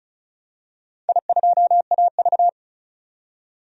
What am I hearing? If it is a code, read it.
Morse code, heard as a keyed tone, about 36 wpm, 700 Hz. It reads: I2AV
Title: DXCC-100 Call Signs - 4 Characters - 35wpm